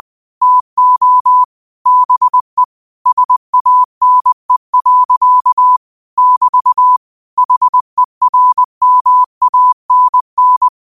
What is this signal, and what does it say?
Morse code, heard as a keyed tone, about 20 wpm, 1000 Hz. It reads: TO BE SANE. <BT> HERMANN
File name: qotdfiles / 20wpm